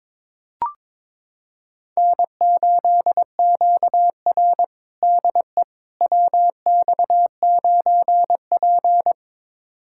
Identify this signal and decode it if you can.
Morse code, heard as a keyed tone, about 22 wpm, 700 Hz. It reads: N8QR DE WX9P
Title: USA Callsign DE Callsign - 4 Characters - 22wpm